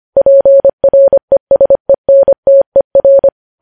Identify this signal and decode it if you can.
Morse code, heard as a keyed tone, about 25 wpm, 550 Hz. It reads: PRESENTER
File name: podcast / News headlines-1x-25x25